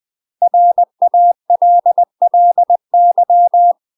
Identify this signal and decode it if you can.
Morse code, heard as a keyed tone, about 20 wpm, 700 Hz. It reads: RALLY
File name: UK MASTER QSOs TRAINING / 20WPM / UK MASTER QSOs 0064